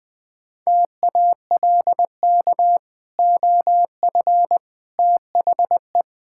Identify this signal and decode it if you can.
Morse code, heard as a keyed tone, about 20 wpm, 700 Hz. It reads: TALK OF THE
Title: Build A Phrase - 4-Word Phrases - 20wpm